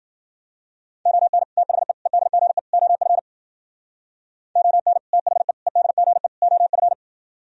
Morse code, heard as a keyed tone, about 53 wpm, 700 Hz. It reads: ON THE EDGE OF ON THE EDGE OF